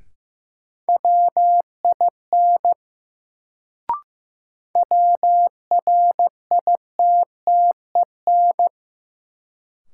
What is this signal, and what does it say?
Morse code, heard as a keyed tone, about 15 wpm, 700 Hz. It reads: WIN WRITTEN
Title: Sentences from Top 1500 Words - New Word Drill Alt - 15wpm